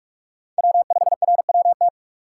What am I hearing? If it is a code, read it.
Morse code, heard as a keyed tone, about 46 wpm, 700 Hz. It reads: W5RWT
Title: USA Call Signs - 5 Characters - 2x Character Spacing - 45wpm